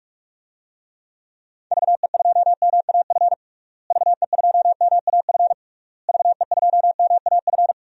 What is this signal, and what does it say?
Morse code, heard as a keyed tone, about 41 wpm, 700 Hz. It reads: VE2MAF VE2MAF VE2MAF
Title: DXCC-100 Call Signs - 6 Characters - Repeated 3x - 45wpm